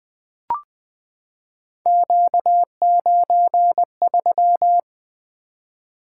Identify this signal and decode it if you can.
Morse code, heard as a keyed tone, about 20 wpm, 700 Hz. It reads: Q93
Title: Three Letter-Number Combinations - 20wpm